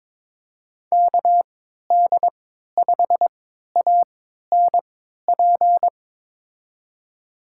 Morse code, heard as a keyed tone, about 22 wpm, spaced as for 12 wpm, 700 Hz. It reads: KD5ANP
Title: USA Call Signs - 3x Character Spacing - 22wpm